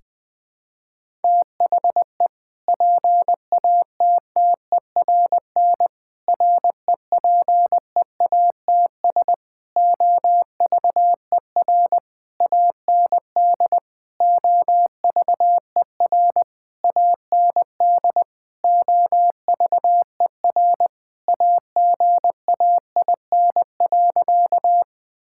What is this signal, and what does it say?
Morse code, heard as a keyed tone, about 20 wpm, 700 Hz. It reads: THE PATTERN REPEATS OVER AND OVER AND OVER AGAIN.
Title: Sentences with 6-Word Phrases - 20wpm